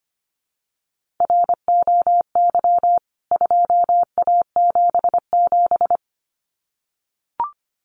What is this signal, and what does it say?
Morse code, heard as a keyed tone, about 25 wpm, 700 Hz. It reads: ROY 2A77
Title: CWT Exchange 25wpm